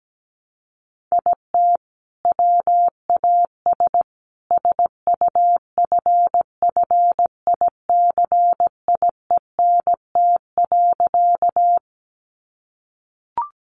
Morse code, heard as a keyed tone, about 17 wpm, 700 Hz. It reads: IT WAS SUFFICIENT.
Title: Sentences from Top 1400 Words 17wpm